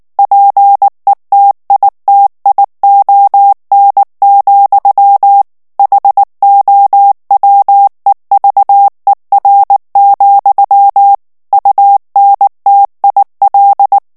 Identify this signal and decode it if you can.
Morse code, heard as a keyed tone, about 19 wpm, 800 Hz. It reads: PETITION, HOWEVER, UNTIL
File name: ANGLAIS texte clair 3 vitesse 18 mots minute